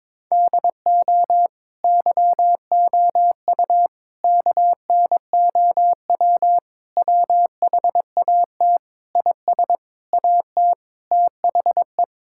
Morse code, heard as a keyed tone, about 22 wpm, 700 Hz. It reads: DO YOU KNOW WHAT IS AT THE
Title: Sentences from Top 1000 Words - Review Part 6 - 22wpm